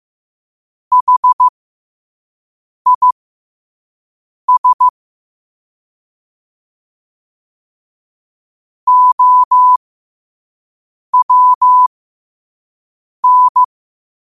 Morse code, heard as a keyed tone, about 15 wpm, spaced as for 5 wpm, 1000 Hz. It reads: HIS OWN
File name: qotdfiles / 5wpm